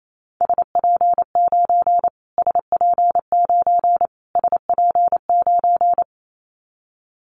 Morse code, heard as a keyed tone, about 28 wpm, 700 Hz. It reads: SP9 SP9 SP9
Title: Callsign Prefixes for Top 25 DXCC - Repeated 3x - 28wpm